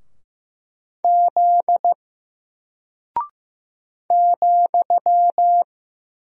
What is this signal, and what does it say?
Morse code, heard as a keyed tone, about 15 wpm, 700 Hz. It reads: Z ,
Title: Single Character (Letter:Number:Punctuation:Prosign) Alt 15wpm